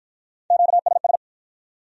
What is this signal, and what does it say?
Morse code, heard as a keyed tone, about 53 wpm, 700 Hz. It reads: QSS